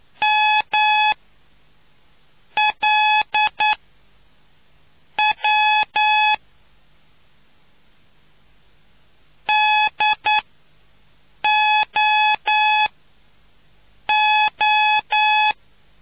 Morse code, heard as a keyed tone, about 9 wpm, spaced as for 5 wpm, 850 Hz. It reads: MLW DOO